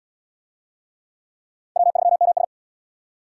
Morse code, heard as a keyed tone, about 77 wpm, 700 Hz. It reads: W3MS